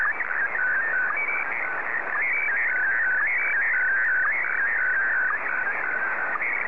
SSTV (slow-scan television), Scottie DX.